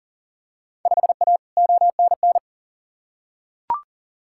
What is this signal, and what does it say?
Morse code, heard as a keyed tone, about 40 wpm, 700 Hz. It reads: 5A ONN
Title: Field Day Exchanges Encore - 40wpm